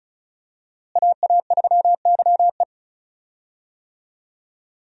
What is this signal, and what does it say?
Morse code, heard as a keyed tone, about 34 wpm, 700 Hz. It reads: AA3YE
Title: USA Call Signs - 5 Characters - Repeated 3x - 35wpm